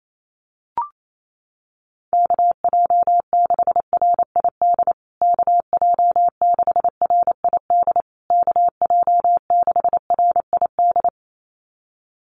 Morse code, heard as a keyed tone, about 28 wpm, 700 Hz. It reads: KJ6RID KJ6RID KJ6RID
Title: USA Call Signs Repeated 3x 28wpm